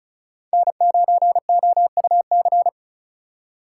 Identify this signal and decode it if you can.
Morse code, heard as a keyed tone, about 36 wpm, 700 Hz. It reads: N9OUC
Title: USA Call Signs - 5 Characters - 3x Character Spacing - 35wpm